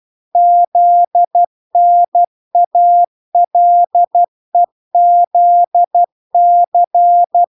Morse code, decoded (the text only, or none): ZNALEZC